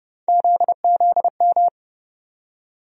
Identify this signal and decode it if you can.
Morse code, heard as a keyed tone, about 31 wpm, 700 Hz. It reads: ZZM
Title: USA Call Sign Postfixes - 3 Letters 30wpm